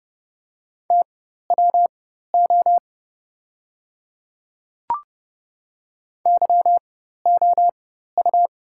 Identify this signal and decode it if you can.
Morse code, heard as a keyed tone, about 30 wpm, spaced as for 14 wpm, 700 Hz. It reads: TWO YOU
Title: 4x Character Spacing - Top 100 Words - 30wpm